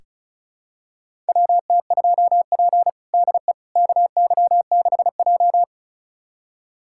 Morse code, heard as a keyed tone, about 36 wpm, 700 Hz. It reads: WT2P DE KY6J
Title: USA Callsign DE Callsign - 4 Characters - 35wpm